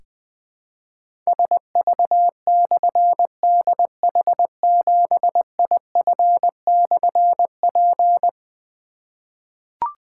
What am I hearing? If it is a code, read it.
Morse code, heard as a keyed tone, about 20 wpm, 700 Hz. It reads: SV/DH7IF/P
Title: Difficult Call Signs 20wpm